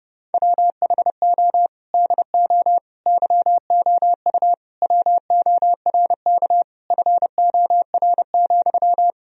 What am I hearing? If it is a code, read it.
Morse code, heard as a keyed tone, about 30 wpm, 700 Hz. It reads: WHO DO YOU WORK FOR,
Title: Sentences from Top 1000 Words - Review Part 4 - 30wpm